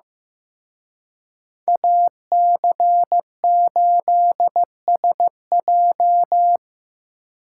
Morse code, heard as a keyed tone, about 18 wpm, 700 Hz. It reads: AC8SJ
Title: Speed Racing - USA 2x2 Call Signs 15wpm